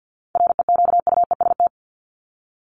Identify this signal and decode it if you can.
Morse code, heard as a keyed tone, about 50 wpm, 700 Hz. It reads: REQUEST
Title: Top 1751-2000 Words 50wpm